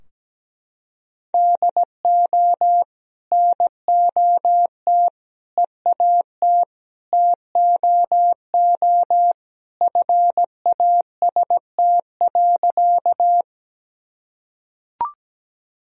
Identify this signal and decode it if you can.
Morse code, heard as a keyed tone, about 17 wpm, 700 Hz. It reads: DO NOT EAT TOO FAST.
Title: Sentences from Top 1000 Words - Review Part 2 - 17wpm